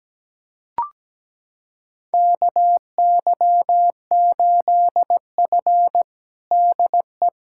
Morse code, heard as a keyed tone, about 17 wpm, 700 Hz. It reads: KY8F DE
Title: USA Callsign DE Callsign - 4 Characters - 17wpm